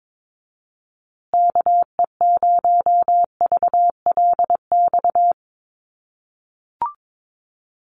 Morse code, heard as a keyed tone, about 22 wpm, 700 Hz. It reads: KE0VLX
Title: USA Call Signs Repeated 3x 22wpm